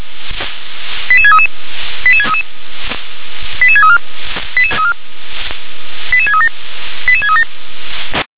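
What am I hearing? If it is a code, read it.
Five-tone selective call (ZVEI), heard as quick runs of five tones, tones of about 70 ms. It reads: 88633 88633 88634 88637 88637
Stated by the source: ZVEI five-tone selective call received on SDR.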